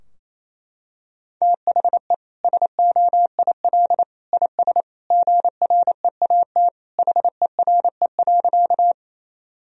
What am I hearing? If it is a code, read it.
Morse code, heard as a keyed tone, about 28 wpm, 700 Hz. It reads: THE SOIL IS GREAT HERE.
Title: Sentences from Top 1000 Words - Review Part 4 - 28wpm